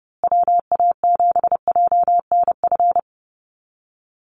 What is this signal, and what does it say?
Morse code, heard as a keyed tone, about 30 wpm, 700 Hz. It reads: WA7JNF
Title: USA Call Signs Repeated 3x 30wpm